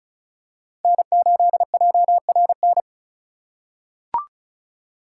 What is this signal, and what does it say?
Morse code, heard as a keyed tone, about 35 wpm, 690 Hz. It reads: N8JRN
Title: USA Call Signs - 5 Characters - 2x Character Spacing - 35wpm